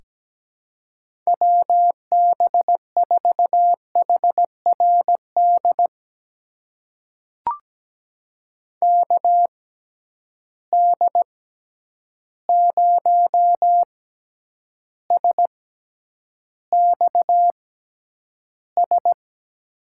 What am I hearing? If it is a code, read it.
Morse code, heard as a keyed tone, about 17 wpm, spaced as for 9 wpm, 700 Hz. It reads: WB4HRD KD0SXS
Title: USA Call Signs - 6 Characters - 6x Character Spacing - 17wpm